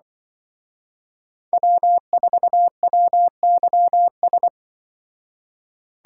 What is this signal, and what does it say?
Morse code, heard as a keyed tone, about 24 wpm, 700 Hz. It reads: W4WYS
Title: Speed Racing - USA 1x3 Call Signs 20wpm